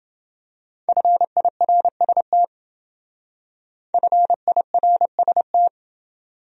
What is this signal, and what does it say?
Morse code, heard as a keyed tone, about 28 wpm, 700 Hz. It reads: FIRST FIRST